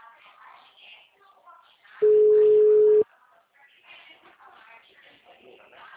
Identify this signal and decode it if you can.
Ringback tone (Europe, 425 Hz).